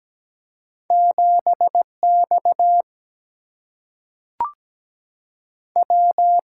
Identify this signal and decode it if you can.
Morse code, heard as a keyed tone, about 17 wpm, 700 Hz. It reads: 7 X W
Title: Single-Letter Single-Number 17wpm